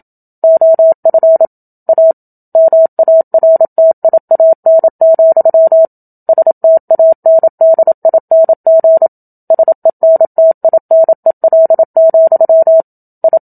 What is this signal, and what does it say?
Morse code, heard as a keyed tone, about 27 wpm, 650 Hz. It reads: OF A MARTIAN, STANDING SENTINEL, I
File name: Book Two / 01-Chapter 1 - 27 WPM